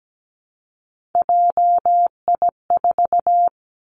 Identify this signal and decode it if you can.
Morse code, heard as a keyed tone, about 17 wpm, 700 Hz. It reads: JI4
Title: Three Challenging Letter-Number Combinations 17wpm